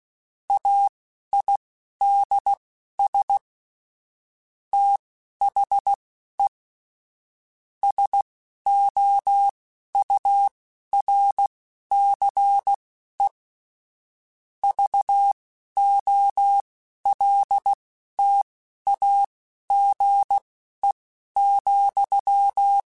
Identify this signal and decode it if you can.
Morse code, heard as a keyed tone, about 16 wpm, spaced as for 11 wpm, 800 Hz. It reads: AIDS THE SOURCE VOLTAGE,
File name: Technical Subjects / Technical Subjects-mp3s / XL @10